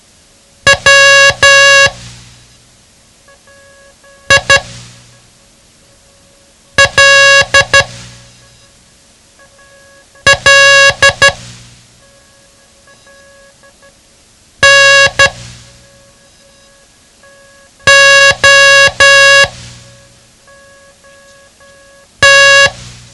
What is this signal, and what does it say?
Morse code, heard as a keyed tone, about 12 wpm, spaced as for 3 wpm, 1650 Hz. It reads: WILL NOT
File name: ham radio / CodeQuick / Code Quick 07